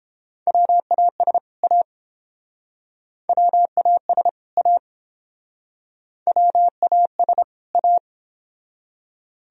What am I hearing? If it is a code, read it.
Morse code, heard as a keyed tone, about 30 wpm, 700 Hz. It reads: WAS A WAS A WAS A